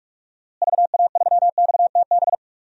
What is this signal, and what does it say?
Morse code, heard as a keyed tone, about 46 wpm, 700 Hz. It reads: VA3XTB
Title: DXCC-100 Call Signs - 6 Characters - Repeated 3x - 45wpm